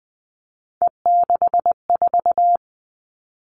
Morse code, heard as a keyed tone, about 19 wpm, 700 Hz. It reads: E64
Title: Three Challenging Letter-Number Combinations 20wpm